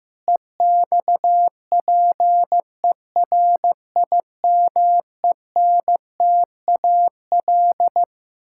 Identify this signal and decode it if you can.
Morse code, heard as a keyed tone, about 14 wpm, 700 Hz. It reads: EXPERIMENTAL
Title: Pangram Challenge v2 - 15wpm